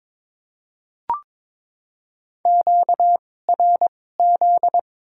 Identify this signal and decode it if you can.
Morse code, heard as a keyed tone, about 21 wpm, spaced as for 16 wpm, 700 Hz. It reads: QRZ
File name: Top 90 QSO Elements - 2x Character Spacing - 22wpm